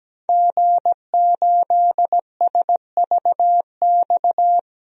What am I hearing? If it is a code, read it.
Morse code, heard as a keyed tone, about 17 wpm, 700 Hz. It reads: G8SVX
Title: DXCC-100 Call Signs - 5 Characters - 17wpm